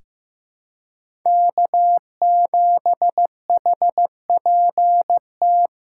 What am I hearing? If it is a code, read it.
Morse code, heard as a keyed tone, about 15 wpm, 700 Hz. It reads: K7HPT
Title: K1USN SST QSO Practice - 15wpm